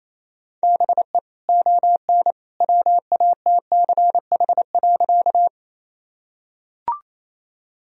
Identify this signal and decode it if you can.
Morse code, heard as a keyed tone, about 28 wpm, 700 Hz. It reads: BE ON WATCH.
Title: Sentences from Top 1000 Words - Review Part 1 - 28wpm